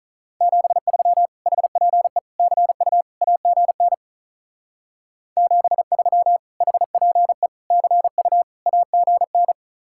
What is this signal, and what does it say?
Morse code, heard as a keyed tone, about 42 wpm, 700 Hz. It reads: 73 HPE CU AGN 73 HPE CU AGN
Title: Speed-Racing Common QSO Phrases Encore - 35wpm